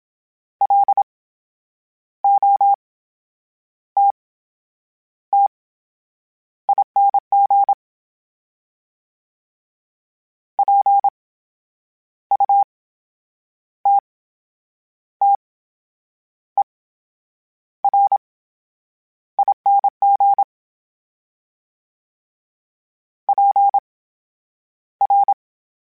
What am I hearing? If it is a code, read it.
Morse code, heard as a keyed tone, about 27 wpm, spaced as for 8 wpm, 800 Hz. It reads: LOTTING PUTTERING PR
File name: Special INGs / Special INGs-mp3s / VerySlowToFaster Ing @25